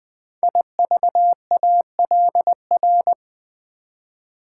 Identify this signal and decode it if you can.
Morse code, heard as a keyed tone, about 20 wpm, 700 Hz. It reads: IVALR
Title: Code Groups - 5 Letters - 20wpm